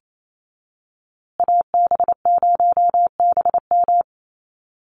Morse code, heard as a keyed tone, about 28 wpm, 700 Hz. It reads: AB0BM
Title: USA Call Signs Repeated 2x 28wpm